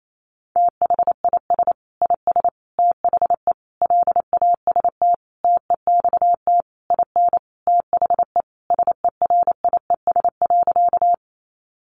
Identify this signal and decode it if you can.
Morse code, heard as a keyed tone, about 28 wpm, 700 Hz. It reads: THIS IS THE LAST TEXT IN THE SERIES.
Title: Sentences from Top 1200 Words 28wpm